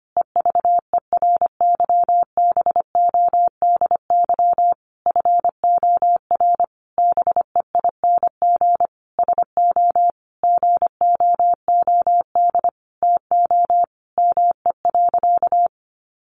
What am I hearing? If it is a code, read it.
Morse code, heard as a keyed tone, about 25 wpm, 700 Hz. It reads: EVERYBODY FOR BEING SO GOOD TO ME.
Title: Sentences from Top 1400 Words 25wpm